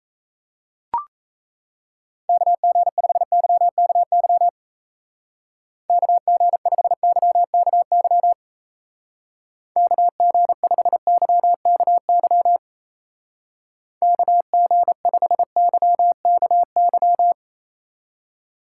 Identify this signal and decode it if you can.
Morse code, heard as a keyed tone, about 36 wpm, 700 Hz. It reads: KG5YKY KG5YKY KG5YKY KG5YKY